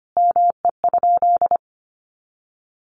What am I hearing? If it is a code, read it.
Morse code, heard as a keyed tone, about 25 wpm, 700 Hz. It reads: ME?
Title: Sentences from Top 1200 Words 25wpm